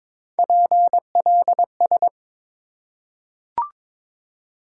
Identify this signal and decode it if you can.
Morse code, heard as a keyed tone, about 23 wpm, 700 Hz. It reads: PLS
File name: Top 90 QSO Elements - 8x Character Spacing - 22wpm